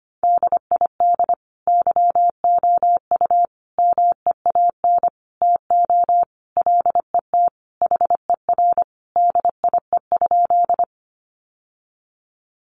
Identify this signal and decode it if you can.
Morse code, heard as a keyed tone, about 25 wpm, 700 Hz. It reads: DID YOU MEAN TO LET HER DIE?
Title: Sentences from Top 1300 Words 25wpm